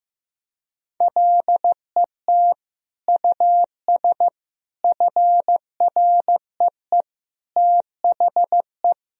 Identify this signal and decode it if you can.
Morse code, heard as a keyed tone, about 16 wpm, 700 Hz. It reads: LET US FREE THE
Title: Sentences from Top 1000 Words - Review Part 1 - 15wpm